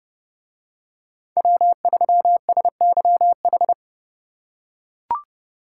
Morse code, heard as a keyed tone, about 27 wpm, 700 Hz. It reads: W3SYH